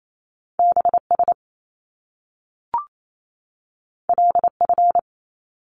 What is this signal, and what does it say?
Morse code, heard as a keyed tone, about 28 wpm, 700 Hz. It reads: BS LF
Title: Top Two Letter Combinations in Words 28wpm